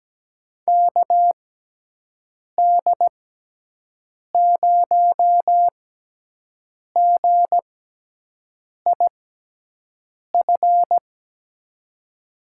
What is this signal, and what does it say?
Morse code, heard as a keyed tone, about 17 wpm, spaced as for 6 wpm, 700 Hz. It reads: KD0GIF
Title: USA Call Signs - 6x Character Spacing - 17wpm